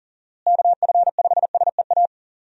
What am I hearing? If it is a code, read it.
Morse code, heard as a keyed tone, about 40 wpm, 700 Hz. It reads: KF5SEA
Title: USA Call Signs - 6 Characters - Repeated 3x - 40wpm